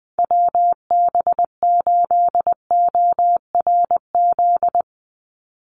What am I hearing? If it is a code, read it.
Morse code, heard as a keyed tone, about 20 wpm, 700 Hz. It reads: WB8ORZ
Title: USA Call Signs 20wpm